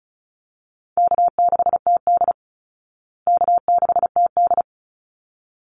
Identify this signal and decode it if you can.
Morse code, heard as a keyed tone, about 34 wpm, 700 Hz. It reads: K6TD K6TD
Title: Easy Call Signs Repeated 2x - 35wpm